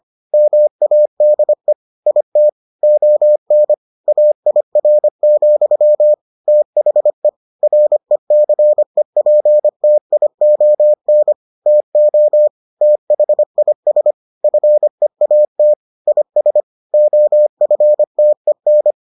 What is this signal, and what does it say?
Morse code, heard as a keyed tone, about 25 wpm, 600 Hz. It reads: MADE IT ON AIR, THE RECEPTION TO THIS FEAT IS OFTEN